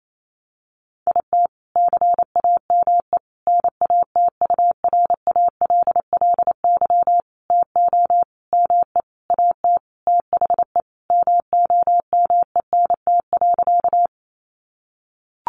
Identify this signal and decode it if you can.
Morse code, heard as a keyed tone, about 28 wpm, 700 Hz. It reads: IT CAME NATURALLY TO ME AT THE MOMENT.
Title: Sentences from Top 1300 Words 28wpm